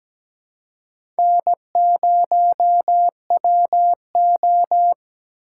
Morse code, heard as a keyed tone, about 17 wpm, 700 Hz. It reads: N0WO
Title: USA Call Signs - 4 Characters - 6x Character Spacing - 17wpm